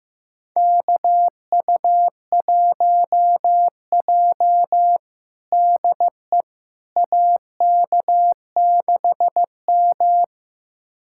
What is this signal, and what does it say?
Morse code, heard as a keyed tone, about 15 wpm, 700 Hz. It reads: KU1J DE AK6M
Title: USA Callsign DE Callsign - 4 Characters - 15wpm